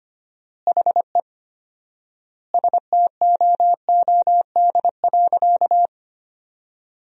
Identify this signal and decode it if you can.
Morse code, heard as a keyed tone, about 25 wpm, 700 Hz. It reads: HE STOOD.
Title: 4x Word-Spacing - Sentences from Top 500 Words - 25wpm